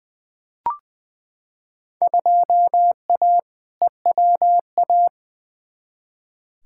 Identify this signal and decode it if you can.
Morse code, heard as a keyed tone, about 20 wpm, 700 Hz. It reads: 2A EWA
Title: Field Day Exchanges - 20wpm